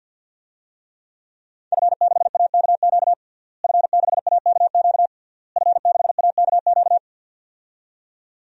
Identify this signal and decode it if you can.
Morse code, heard as a keyed tone, about 46 wpm, 700 Hz. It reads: F6AKQ F6AKQ F6AKQ
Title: DXCC-100 Call Signs - 5 Characters - Repeated 3x - 50wpm